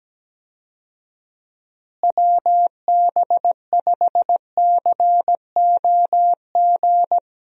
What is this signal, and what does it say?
Morse code, heard as a keyed tone, about 17 wpm, 700 Hz. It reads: WB5COG
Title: USA Call Signs - 6 Characters - Repeated 3x - 17wpm